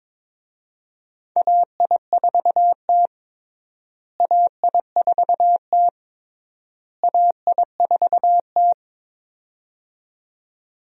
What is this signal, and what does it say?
Morse code, heard as a keyed tone, about 22 wpm, 700 Hz. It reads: AI4T AI4T AI4T
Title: USA Call Signs - 4 Characters - Repeated 3x - 22wpm